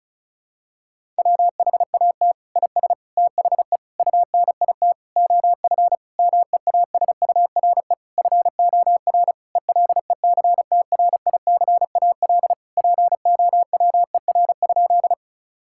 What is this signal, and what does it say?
Morse code, heard as a keyed tone, about 36 wpm, 700 Hz. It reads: WHAT IS THE UNIT OF MEASURE FOR ELECTRICAL POWER?
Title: Sentences from Top 1000 Words - Review Part 6 - 35wpm